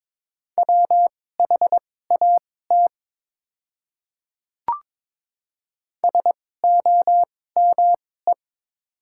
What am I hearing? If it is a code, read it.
Morse code, heard as a keyed tone, about 22 wpm, spaced as for 16 wpm, 700 Hz. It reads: WHAT SOME
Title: 2x Character Spacing - Top 200 Words - 22wpm